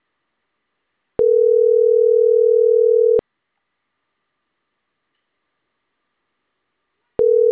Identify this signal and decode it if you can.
Ringback tone (North America), 440 + 480 Hz.